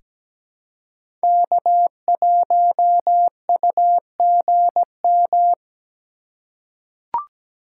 Morse code, heard as a keyed tone, about 17 wpm, 700 Hz. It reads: K1UGM
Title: USA Call Signs - 5 Characters - 2x Character Spacing - 17wpm